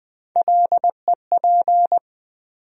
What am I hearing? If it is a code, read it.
Morse code, heard as a keyed tone, about 20 wpm, 700 Hz. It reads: LEP
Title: USA Call Sign Postfixes - 3 Letters 20wpm